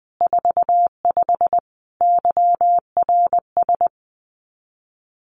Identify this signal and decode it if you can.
Morse code, heard as a keyed tone, about 20 wpm, 700 Hz. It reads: 45 YRS
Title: Common QSO Phrases Encore 20wpm